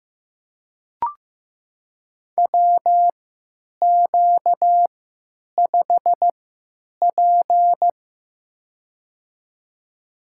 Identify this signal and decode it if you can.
Morse code, heard as a keyed tone, about 15 wpm, spaced as for 9 wpm, 700 Hz. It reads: WQ5P
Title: USA Call Signs - 4 Characters - 3x Character Spacing - 15wpm